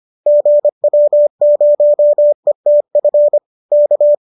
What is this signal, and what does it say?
Morse code, heard as a keyed tone, about 25 wpm, 590 Hz. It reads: GW0ETF K
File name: qso218 25